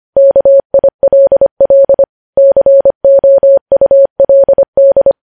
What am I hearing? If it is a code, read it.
Morse code, heard as a keyed tone, about 25 wpm, 550 Hz. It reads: KILL COULD
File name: podcast / News headlines-1x-25x25